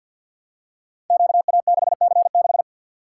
Morse code, heard as a keyed tone, about 50 wpm, 700 Hz. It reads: QA6C6
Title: Code Groups - 5 Letter:Numbers - 50wpm